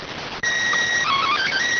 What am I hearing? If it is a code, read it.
SSTV (slow-scan television), PD 290.